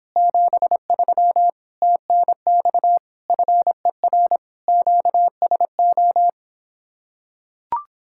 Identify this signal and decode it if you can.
Morse code, heard as a keyed tone, about 26 wpm, 700 Hz. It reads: 73 TNX FER QSO